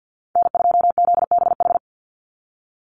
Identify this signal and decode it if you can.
Morse code, heard as a keyed tone, about 50 wpm, 700 Hz. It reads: N2ZBH
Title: USA Call Signs 50wpm